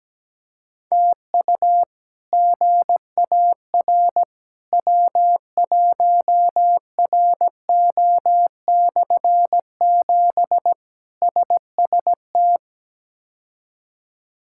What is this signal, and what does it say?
Morse code, heard as a keyed tone, about 17 wpm, 700 Hz. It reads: TU GAR W1RO/7 SST
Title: K1USN SST QSO Practice - 17wpm